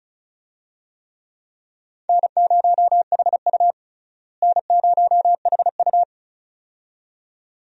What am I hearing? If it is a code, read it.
Morse code, heard as a keyed tone, about 34 wpm, 700 Hz. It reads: N0HU N0HU
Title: USA Call Signs - 4 Characters - Repeated 2x - 35wpm